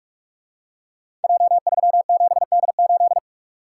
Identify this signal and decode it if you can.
Morse code, heard as a keyed tone, about 44 wpm, 700 Hz. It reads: J37D8
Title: Code Groups - 5 Letter:Numbers - 45wpm